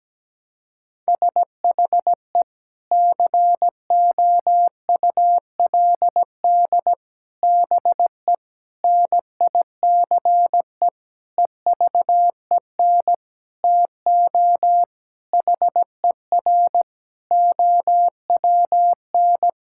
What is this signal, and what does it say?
Morse code, heard as a keyed tone, about 17 wpm, 700 Hz. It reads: SHE COULD BE NICE EVEN TO HER OWN
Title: Sentences from Top 1600 Words 17wpm